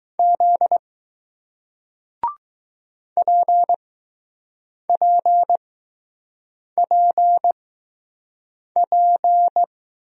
Morse code, heard as a keyed tone, about 20 wpm, spaced as for 6 wpm, 700 Hz. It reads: Z PPPP